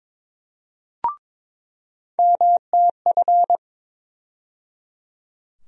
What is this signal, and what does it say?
Morse code, heard as a keyed tone, about 21 wpm, 700 Hz. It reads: MTF
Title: USA Call Sign Postfixes - 3 Letters 22wpm